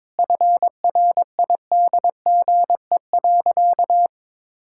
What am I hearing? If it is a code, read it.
Morse code, heard as a keyed tone, about 22 wpm, 700 Hz. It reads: FRIDGE.
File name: practice / 22wpm / Sentences with 4-Word Phrases - 22wpm